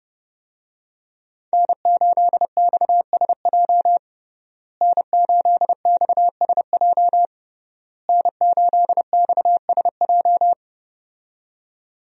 Morse code, heard as a keyed tone, about 32 wpm, 700 Hz. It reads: N8XSJ N8XSJ N8XSJ
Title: USA Call Signs - 5 Characters - Repeated 3x - 30wpm